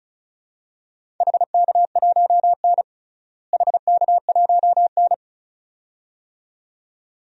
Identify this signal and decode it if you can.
Morse code, heard as a keyed tone, about 36 wpm, 700 Hz. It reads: HK1N HK1N
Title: DXCC-100 Call Signs - 4 Characters - Repeated 2x - 35wpm